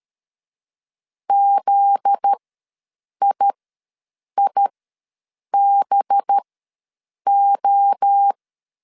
Morse code, heard as a keyed tone, about 13 wpm, spaced as for 7 wpm, 800 Hz. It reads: ZIIBO
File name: 7 GF9Z ws 12wpm